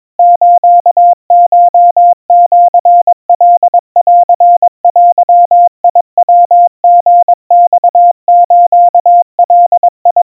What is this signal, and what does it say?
Morse code (Japanese Wabun code), heard as a keyed tone, about 22 wpm, 700 Hz. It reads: スコシカンデヤリマスガ
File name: 80 cpm 700 Hz Crispy